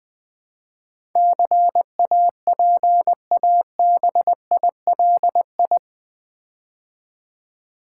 Morse code, heard as a keyed tone, about 20 wpm, 700 Hz. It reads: CAPABILI
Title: Build A Word - 4 Letters At A Time - 10-Letter Words - 20wpm